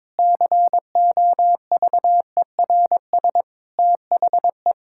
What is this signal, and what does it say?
Morse code, heard as a keyed tone, about 22 wpm, 700 Hz. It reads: COVERS THE
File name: practice / 22wpm / Sentences with 4-Word Phrases - 22wpm